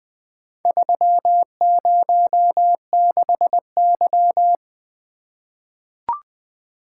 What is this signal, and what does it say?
Morse code, heard as a keyed tone, about 20 wpm, 700 Hz. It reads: 306Y
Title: Four Letter-Number Combinations - 20wpm